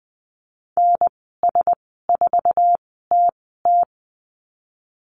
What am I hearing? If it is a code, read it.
Morse code, heard as a keyed tone, about 20 wpm, spaced as for 14 wpm, 700 Hz. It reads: NS4TT
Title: 055 Lesson 10g - Letter r - Call Signs 20wpm